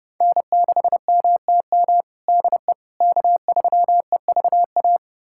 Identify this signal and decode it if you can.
Morse code, heard as a keyed tone, about 30 wpm, 700 Hz. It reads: N6MTM DE K3EVA
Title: USA Callsign DE Callsign - 5 Characters - 30wpm